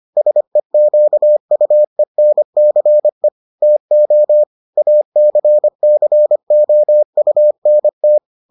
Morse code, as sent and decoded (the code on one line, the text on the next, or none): ... --.- ..- . -. -.-. / - --- / .- -.-. -.-. --- ..- -. -
SEQUENCE TO ACCOUNT